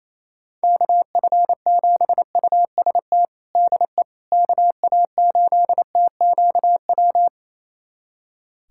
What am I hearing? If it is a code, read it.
Morse code, heard as a keyed tone, about 28 wpm, 700 Hz. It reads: KF7UST DE KA8TQW
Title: USA Callsign DE Callsign 28wpm